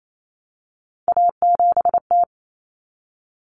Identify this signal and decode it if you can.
Morse code, heard as a keyed tone, about 28 wpm, 700 Hz. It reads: A7T
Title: Common Cut Numbers - 3-Digit 28wpm